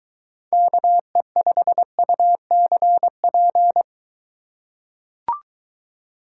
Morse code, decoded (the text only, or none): KE5UCP